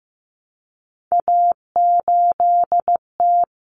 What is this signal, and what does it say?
Morse code, heard as a keyed tone, about 15 wpm, 700 Hz. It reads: A8T
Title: Common Cut Numbers - 3-Digit 15wpm